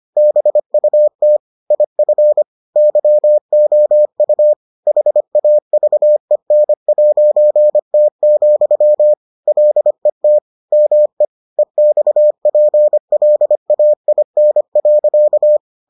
Morse code, heard as a keyed tone, about 25 wpm, 600 Hz. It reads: BUT IF YOU HAVEN'T, LET ME EXPLAIN.